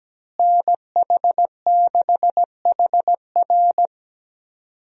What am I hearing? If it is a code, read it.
Morse code, heard as a keyed tone, about 17 wpm, 700 Hz. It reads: NH6HR
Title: USA Call Signs - 5 Characters - Repeated 3x - 17wpm